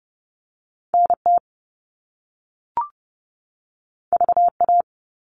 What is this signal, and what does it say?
Morse code, heard as a keyed tone, about 30 wpm, 700 Hz. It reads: NT VA
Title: All Cut Numbers - 2-Digit 30wpm